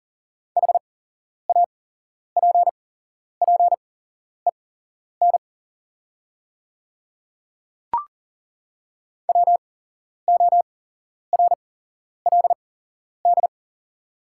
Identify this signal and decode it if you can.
Morse code, heard as a keyed tone, about 40 wpm, spaced as for 11 wpm, 690 Hz. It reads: HAPPEN WORLD
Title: 8x Character Spacing - Top 500 Words - 40wpm